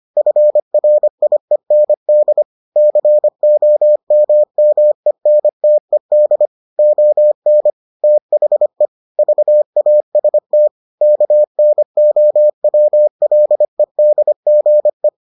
Morse code, heard as a keyed tone, about 25 wpm, 600 Hz. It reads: FRIEND COMMENTED ON THE VAST KNOWLEDGE